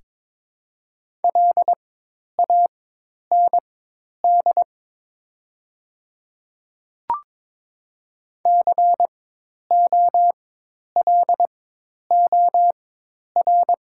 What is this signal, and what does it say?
Morse code, heard as a keyed tone, about 22 wpm, spaced as for 10 wpm, 700 Hz. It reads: LAND COLOR
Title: 4x Character Spacing - Top 300 Words - 22wpm